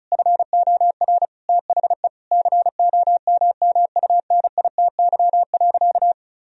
Morse code, heard as a keyed tone, about 34 wpm, 700 Hz. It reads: FOR THE COMMUNITY.
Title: Sentences from Top 1000 Words - Review Part 4 - 35wpm